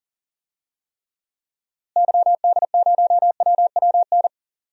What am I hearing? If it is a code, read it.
Morse code, heard as a keyed tone, about 40 wpm, 700 Hz. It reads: YD0WWN
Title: DXCC-100 Call Signs - 6 Characters - Repeated 3x - 40wpm